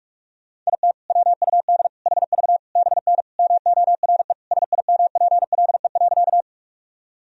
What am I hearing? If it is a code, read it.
Morse code, heard as a keyed tone, about 41 wpm, 700 Hz. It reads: IT WUD HV BN MORE SIMPLE.
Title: Sentences from Top 400 Words and Abbreviations 45wpm